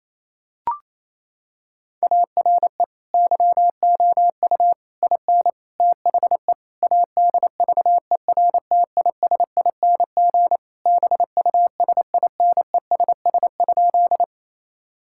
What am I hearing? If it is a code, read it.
Morse code, heard as a keyed tone, about 28 wpm, 700 Hz. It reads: ARE YOU IN THE ADVERTISING BUSINESS?
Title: Sentences from Top 1000 Words - Review Part 6 - 28wpm